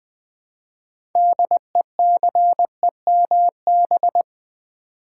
Morse code, heard as a keyed tone, about 20 wpm, 700 Hz. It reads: DECEMB
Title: Build A Word - 2 Letters At A Time - 8-Letter Words - 20wpm